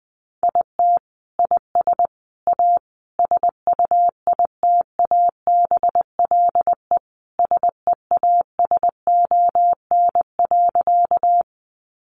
Morse code, heard as a keyed tone, about 20 wpm, 700 Hz. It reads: IT IS A SUITABLE SEASON.
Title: Sentences from Top 1400 Words 20wpm